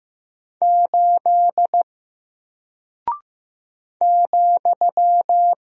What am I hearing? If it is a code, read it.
Morse code, heard as a keyed tone, about 15 wpm, 700 Hz. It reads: 8 ,
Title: Single Character (Letter:Number:Punctuation:Prosign) Alt 15wpm